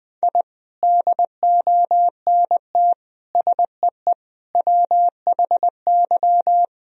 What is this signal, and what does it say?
Morse code, heard as a keyed tone, about 20 wpm, 700 Hz. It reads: I DONT SEE WHY
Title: Sentences from Top 1000 Words - Review Part 2 - 20wpm